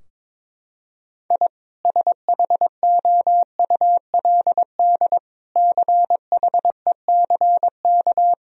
Morse code, heard as a keyed tone, about 22 wpm, 700 Hz. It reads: I SHOULD CHECK
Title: Sentences from Top 1000 Words - Review Part 3 - 22wpm